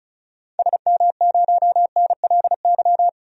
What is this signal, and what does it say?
Morse code, heard as a keyed tone, about 36 wpm, 700 Hz. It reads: SM0NLY
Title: DXCC-100 Call Signs - 6 Characters - 35wpm